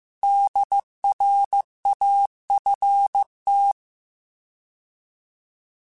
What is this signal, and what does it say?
Morse code, heard as a keyed tone, about 15 wpm, 800 Hz. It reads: DRAFT